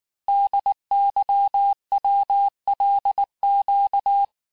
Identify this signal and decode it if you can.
Morse code, heard as a keyed tone, about 19 wpm, 800 Hz. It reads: DYWLQ
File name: cours-du-lundi files / cours-1595-teste-CHIFFRE-vitesse18-MPM